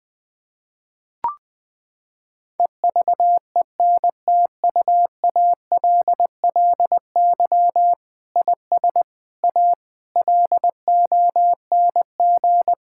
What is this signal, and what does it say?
Morse code, heard as a keyed tone, about 20 wpm, 700 Hz. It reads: EVENTUALLY IS A LONG
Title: Sentences from Top 1000 Words - Review Part 6 - 20wpm